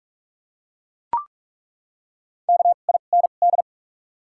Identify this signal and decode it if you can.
Morse code, heard as a keyed tone, about 42 wpm, spaced as for 31 wpm, 700 Hz. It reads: KIND
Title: 2x Character Spacing - Top 200 Words - 45wpm